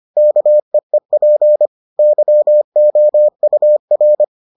Morse code, heard as a keyed tone, about 25 wpm, 600 Hz. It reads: KEEP YOUR